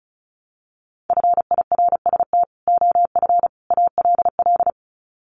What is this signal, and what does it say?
Morse code, heard as a keyed tone, about 34 wpm, 700 Hz. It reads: FIRST OF ALL
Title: Sets of 3 Words 35wpm